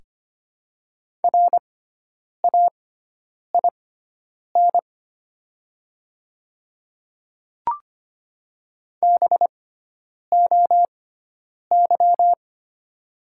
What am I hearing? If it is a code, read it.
Morse code, heard as a keyed tone, about 25 wpm, spaced as for 9 wpm, 700 Hz. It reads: RAIN BOY
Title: 6x Character Spacing - Top 300 Words - 25wpm